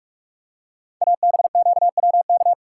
Morse code, heard as a keyed tone, about 44 wpm, 700 Hz. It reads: ABQWK
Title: Code Groups - 5 Letters - 45wpm